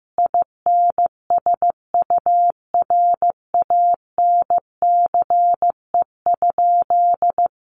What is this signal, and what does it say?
Morse code, heard as a keyed tone, about 15 wpm, 700 Hz. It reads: INSURANCE?
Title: Sentences from Top 1100 Words 15wpm